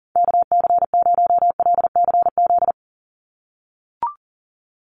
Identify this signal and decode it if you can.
Morse code, heard as a keyed tone, about 40 wpm, 700 Hz. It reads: KC0LCZ
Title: USA Call Signs 40wpm